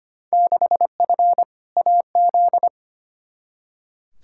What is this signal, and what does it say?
Morse code, heard as a keyed tone, about 25 wpm, 700 Hz. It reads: 6F AZ
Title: Field Day Exchanges Encore - 25wpm